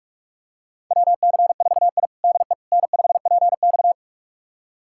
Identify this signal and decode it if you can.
Morse code, heard as a keyed tone, about 46 wpm, 700 Hz. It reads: WC4I DE N5PX
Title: USA Callsign DE Callsign - 4 Characters - 45wpm